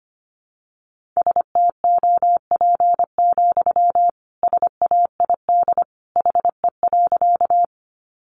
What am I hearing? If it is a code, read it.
Morse code, heard as a keyed tone, about 25 wpm, 700 Hz. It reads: STOP, SAID HE.
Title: Sentences from Top 300 Words 25wpm